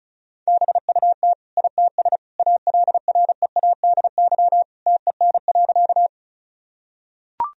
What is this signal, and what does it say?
Morse code, heard as a keyed tone, about 34 wpm, 700 Hz. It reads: BUT ITS ALREADY TEN.
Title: Sentences from Top 1000 Words - Review Part 2 - 35wpm